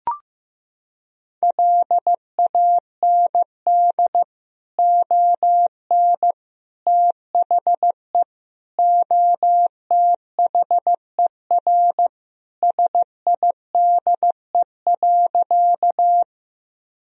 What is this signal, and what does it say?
Morse code, heard as a keyed tone, about 15 wpm, 700 Hz. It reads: LAND ON THE OTHER SIDE.
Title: Sentences from Top 1000 Words - Review Part 1 - 15wpm